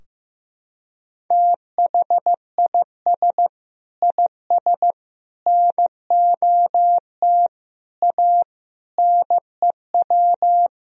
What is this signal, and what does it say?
Morse code, heard as a keyed tone, about 15 wpm, 700 Hz. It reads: THIS IS NOT A NEW
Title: Sentences from Top 1000 Words - Review Part 4 - 15wpm